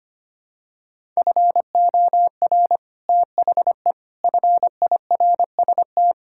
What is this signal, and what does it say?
Morse code, heard as a keyed tone, about 25 wpm, 700 Hz. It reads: FOR THE FIRST
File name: practice / 25wpm / Sentences with 9-Word Phrases - 25wpm